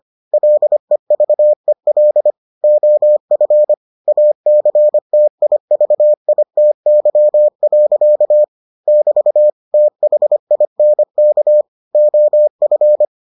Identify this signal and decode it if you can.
Morse code, heard as a keyed tone, about 25 wpm, 600 Hz. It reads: LEVEL OF ACTIVITY. <BT> THINK OF